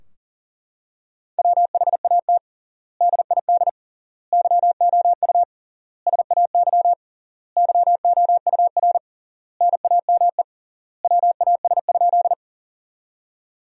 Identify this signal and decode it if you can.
Morse code, heard as a keyed tone, about 40 wpm, 700 Hz. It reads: WHAT DID YOU SAY YOUR NAME WAS?
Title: 3x Word-Spacing - Sentences from Top 200 Words - 40wpm